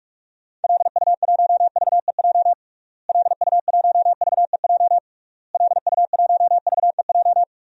Morse code, heard as a keyed tone, about 46 wpm, 700 Hz. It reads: LU1VEJ LU1VEJ LU1VEJ
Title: DXCC-100 Call Signs - 6 Characters - Repeated 3x - 45wpm